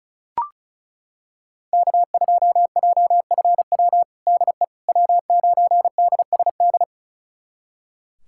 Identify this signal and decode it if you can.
Morse code, heard as a keyed tone, about 34 wpm, 700 Hz. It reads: K2JFW DE W9DSD
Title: USA Callsign DE Callsign - 5 Characters - 35wpm